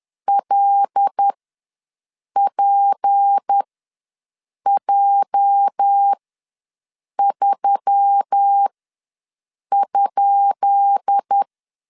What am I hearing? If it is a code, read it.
Morse code, heard as a keyed tone, about 10 wpm, spaced as for 6 wpm, 800 Hz. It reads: LPJ3?
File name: LEZIONI / 10 SIGNS / 10B INT ws 10wpm 4134